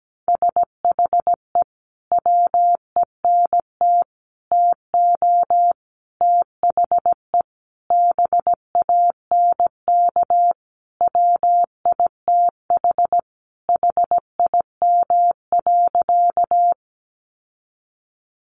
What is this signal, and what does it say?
Morse code, heard as a keyed tone, about 17 wpm, 700 Hz. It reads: SHE WENT TO THE BANK WITH HIM.
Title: Sentences from Top 900 Words 17wpm